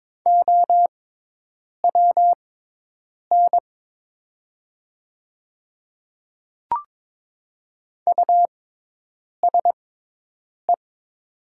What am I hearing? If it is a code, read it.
Morse code, heard as a keyed tone, about 22 wpm, spaced as for 8 wpm, 700 Hz. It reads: OWN USE
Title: 6x Character Spacing - Top 300 Words - 22wpm